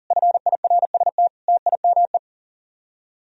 Morse code, heard as a keyed tone, about 51 wpm, 700 Hz. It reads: FIRST TIME